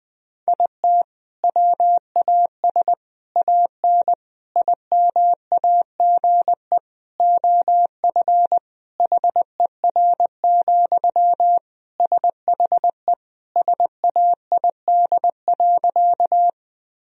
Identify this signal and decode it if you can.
Morse code, heard as a keyed tone, about 20 wpm, 700 Hz. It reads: IT WAS AN IMAGE OF HER, SHE SAID.
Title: Sentences from Top 1000 Words - Review Part 4 - 20wpm